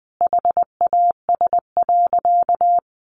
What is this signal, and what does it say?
Morse code, heard as a keyed tone, about 20 wpm, 700 Hz. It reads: HAS.
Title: Sentences from Top 500 Words 20wpm